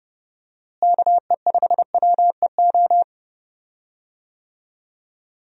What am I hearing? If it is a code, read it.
Morse code, heard as a keyed tone, about 30 wpm, 700 Hz. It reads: KE5WEO
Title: USA Call Signs - 6 Characters - Repeated 2x - 30wpm